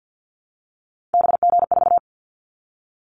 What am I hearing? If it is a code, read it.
Morse code, heard as a keyed tone, about 50 wpm, 690 Hz. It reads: BG4
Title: Callsign Prefixes for Top 100 DXCC - 50wpm